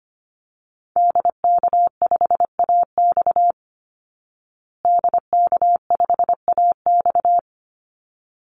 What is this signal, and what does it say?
Morse code, heard as a keyed tone, about 25 wpm, 700 Hz. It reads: DK5AX DK5AX
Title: Call Signs Repeated 3x - 25wpm